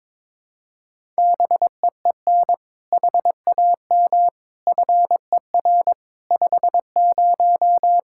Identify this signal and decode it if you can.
Morse code, heard as a keyed tone, about 22 wpm, 700 Hz. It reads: BEEN HAM FER 50
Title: Practice QSO - Exchange 3 - More Info - 22wpm